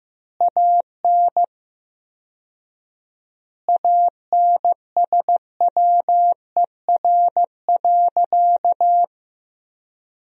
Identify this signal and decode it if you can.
Morse code, heard as a keyed tone, about 15 wpm, 700 Hz. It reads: AN ANSWER.
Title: 4x Word-Spacing - Sentences from Top 300 Words - 15wpm